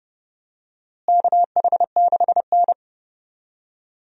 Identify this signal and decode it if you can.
Morse code, heard as a keyed tone, about 30 wpm, 700 Hz. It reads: KH6N
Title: DXCC-100 Call Signs - 4 Characters - Repeated 2x - 30wpm